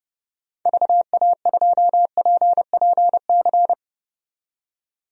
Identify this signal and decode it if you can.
Morse code, heard as a keyed tone, about 30 wpm, 700 Hz. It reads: VA2PPC
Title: DXCC-100 Call Signs - 6 Characters - 30wpm